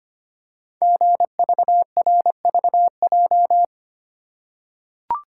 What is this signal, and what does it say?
Morse code, heard as a keyed tone, about 25 wpm, 700 Hz. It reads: GVRVJ
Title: Code Groups - 5 Letter:Numbers - 25wpm